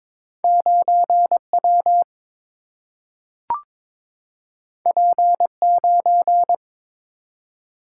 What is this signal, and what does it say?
Morse code, heard as a keyed tone, about 22 wpm, 700 Hz. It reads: 9 W P 9
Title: Single-Letter Single-Number 22wpm